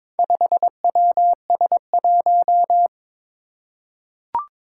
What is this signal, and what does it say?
Morse code, heard as a keyed tone, about 23 wpm, 700 Hz. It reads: 5WS1
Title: Four Challenging Letter-Number Combinations 22wpm